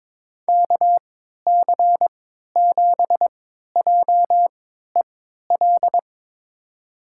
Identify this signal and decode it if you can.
Morse code, heard as a keyed tone, about 21 wpm, spaced as for 12 wpm, 700 Hz. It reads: KC7JEL
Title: USA Call Signs - 3x Character Spacing - 22wpm